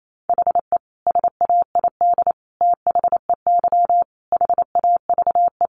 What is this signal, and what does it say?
Morse code, heard as a keyed tone, about 28 wpm, 700 Hz. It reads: HE SAID THEY HAVE
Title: Sentences from Top 800 Words 28wpm